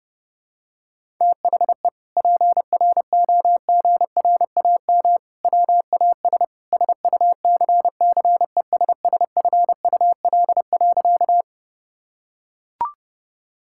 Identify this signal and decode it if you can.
Morse code, heard as a keyed tone, about 30 wpm, 700 Hz. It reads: THE PROGRAM WAS SUCCESSFUL.
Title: Sentences from Top 1000 Words - Review Part 5 - 30wpm